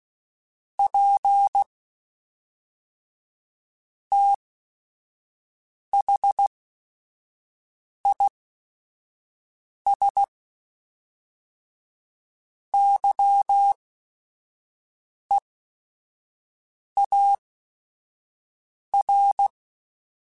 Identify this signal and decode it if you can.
Morse code, heard as a keyed tone, about 16 wpm, spaced as for 5 wpm, 800 Hz. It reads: P THIS YEAR
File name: QSOs / QSOs-mp3s / WD0BGP 2@5